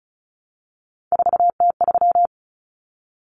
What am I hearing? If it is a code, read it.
Morse code, heard as a keyed tone, about 34 wpm, 700 Hz. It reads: 4T3
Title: Common Cut Numbers - 3-Digit 35wpm